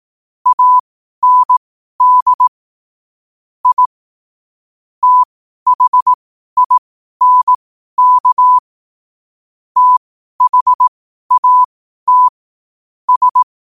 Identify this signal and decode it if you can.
Morse code, heard as a keyed tone, about 18 wpm, spaced as for 12 wpm, 1000 Hz. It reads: AND I THINK THAT S
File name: qotdfiles / 12wpm